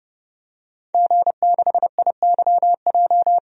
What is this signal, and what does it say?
Morse code, heard as a keyed tone, about 30 wpm, 700 Hz. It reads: G6IYJ
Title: DXCC-100 Call Signs - 5 Characters - Repeated 3x - 30wpm